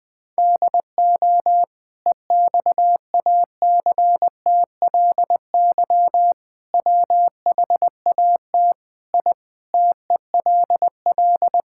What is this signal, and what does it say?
Morse code, heard as a keyed tone, about 21 wpm, 700 Hz. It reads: DO EXACTLY WHAT I TELL
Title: Sentences from Top 1000 Words - Review Part 6 - 20wpm